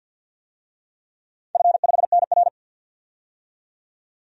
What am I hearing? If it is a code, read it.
Morse code, heard as a keyed tone, about 50 wpm, 700 Hz. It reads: F5NR
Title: DXCC-100 Call Signs - 4 Characters - 50wpm